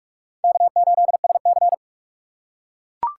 Morse code, heard as a keyed tone, about 43 wpm, 700 Hz. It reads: K8SC
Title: USA Call Signs - 4 Characters - 2x Character Spacing - 45wpm